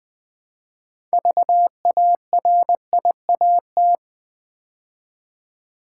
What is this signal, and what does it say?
Morse code, heard as a keyed tone, about 20 wpm, 700 Hz. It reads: VARIAT
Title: Build A Word - 3 Letters At A Time - 9-Letter Words - 20wpm